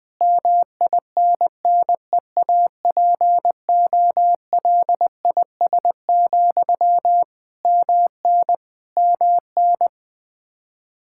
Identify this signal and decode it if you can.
Morse code, heard as a keyed tone, about 21 wpm, 700 Hz. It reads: MINNEAPOLIS, MN MN
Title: Practice QSO - Exchange 1 - Beginning - 20wpm